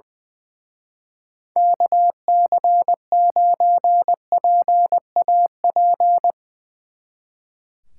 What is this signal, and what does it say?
Morse code, heard as a keyed tone, about 22 wpm, 700 Hz. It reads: KC9PAP
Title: Speed Racing - USA 2x3 Call Signs 20wpm